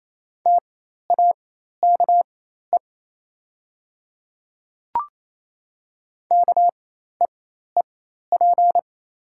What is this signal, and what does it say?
Morse code, heard as a keyed tone, about 28 wpm, spaced as for 13 wpm, 700 Hz. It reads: TAKE KEEP
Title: 4x Character Spacing - Top 500 Words - 28wpm